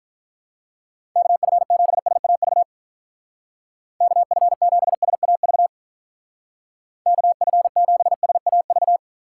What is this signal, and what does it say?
Morse code, heard as a keyed tone, about 45 wpm, 700 Hz. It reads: KF7SAV KF7SAV KF7SAV